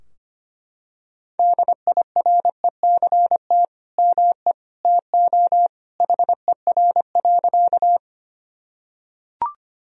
Morse code, heard as a keyed tone, about 25 wpm, 700 Hz. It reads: DIRECT ME TO HER.
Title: Sentences from Top 1000 Words - Review Part 2 - 25wpm